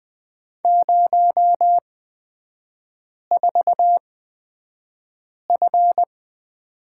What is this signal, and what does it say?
Morse code, heard as a keyed tone, about 20 wpm, spaced as for 5 wpm, 700 Hz. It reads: 04F
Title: 236 Lesson 38e - Letter q - Three Characters 20wpm